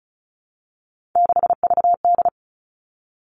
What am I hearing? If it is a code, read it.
Morse code, heard as a keyed tone, about 34 wpm, 700 Hz. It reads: 6VD
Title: All Cut Numbers - 3-Digit 35wpm